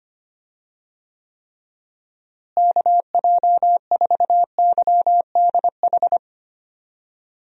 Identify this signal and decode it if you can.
Morse code, heard as a keyed tone, about 25 wpm, 700 Hz. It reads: KJ4YDH
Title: USA Call Signs - 6 Characters - Repeated 2x - 25wpm